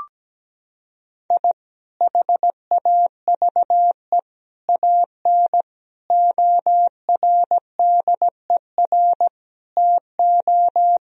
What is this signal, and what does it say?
Morse code, heard as a keyed tone, about 17 wpm, 700 Hz. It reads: I HAVE AN ORDER TO
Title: Sentences from Top 1000 Words - Review Part 2 - 17wpm